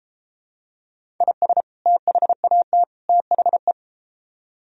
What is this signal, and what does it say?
Morse code, heard as a keyed tone, about 33 wpm, 700 Hz. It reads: IS THAT THE